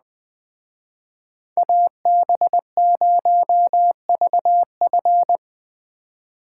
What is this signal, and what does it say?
Morse code, heard as a keyed tone, about 20 wpm, 700 Hz. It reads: AB0VF
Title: Speed Racing - USA 2x2 Call Signs 17wpm